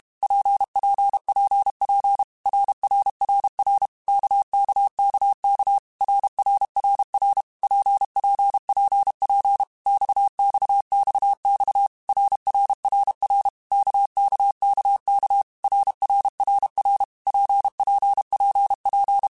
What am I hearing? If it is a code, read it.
Morse code, heard as a keyed tone, about 32 wpm, 800 Hz. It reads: PPPP RRRR KKKK RRRR PPPP XXXX RRRR KKKK RRRR PPPP
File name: Basic Letter Groups By7Groups / RKPX / RKPX@30